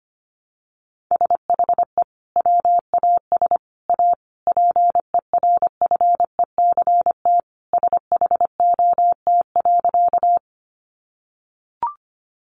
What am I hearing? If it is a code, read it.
Morse code, heard as a keyed tone, about 25 wpm, 700 Hz. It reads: SHE WAS A PERFECT SHOT.
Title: Sentences from Top 1100 Words 25wpm